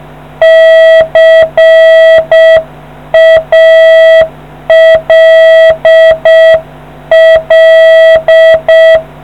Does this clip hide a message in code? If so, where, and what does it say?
Yes: Morse code from 0.4 to 9.0 s, heard as a keyed tone, about 7 wpm, 650 Hz. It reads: CALL